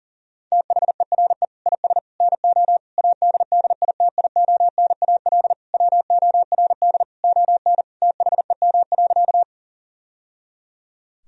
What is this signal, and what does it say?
Morse code, heard as a keyed tone, about 40 wpm, 700 Hz. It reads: THERE IS NO ADDITIONAL WORD ON THEM.
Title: Sentences from Top 1000 Words - Review Part 4 - 40wpm